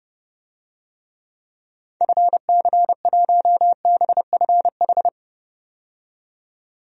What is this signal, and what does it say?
Morse code, heard as a keyed tone, about 32 wpm, 700 Hz. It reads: FC1BFH
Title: DXCC-100 Call Signs - 6 Characters - 30wpm